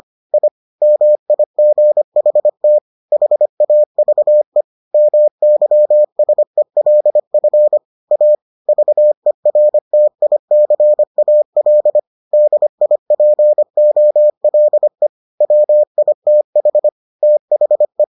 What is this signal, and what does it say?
Morse code, heard as a keyed tone, about 25 wpm, 600 Hz. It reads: I MIGHT HAVE MYSELF A VERTICAL DIPOLE WITH THE